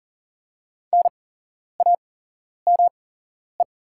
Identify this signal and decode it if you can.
Morse code, heard as a keyed tone, about 40 wpm, spaced as for 11 wpm, 700 Hz. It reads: NAME
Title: 8x Character Spacing - Top 200 Words - 40wpm